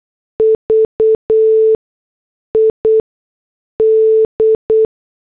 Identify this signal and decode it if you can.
Morse code, heard as a keyed tone, about 8 wpm, spaced as for 6 wpm, 440 Hz. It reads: VID